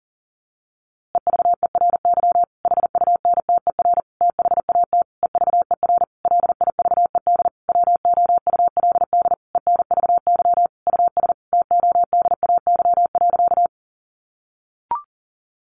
Morse code, heard as a keyed tone, about 40 wpm, 700 Hz. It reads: EVERY HUNTER THAT EVER LIVED WOULD ENVY US TODAY.
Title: A Sound Of Thunder 40wpm